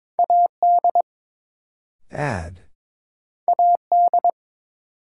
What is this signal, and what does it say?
Morse code, heard as a keyed tone, about 22 wpm, 700 Hz. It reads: AD AD
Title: Sentences from Top 1300 Words - New Word Drill Alt - 22wpm